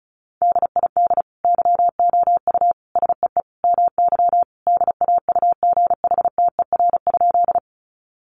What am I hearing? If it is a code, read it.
Morse code, heard as a keyed tone, about 34 wpm, 700 Hz. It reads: DID YOU SEE MY DAUGHTER?
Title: Sentences from Top 1500 Words 35wpm